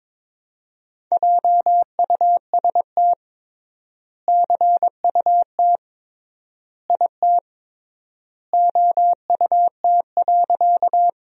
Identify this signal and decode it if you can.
Morse code, heard as a keyed tone, about 21 wpm, 700 Hz. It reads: JUST CUT IT OUT.
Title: 3x Word-Spacing - Sentences from Top 300 Words - 22wpm